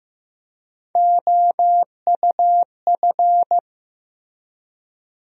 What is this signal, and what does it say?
Morse code, heard as a keyed tone, about 16 wpm, 700 Hz. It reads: OUF
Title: USA Call Sign Postfixes - 3 Letters 15wpm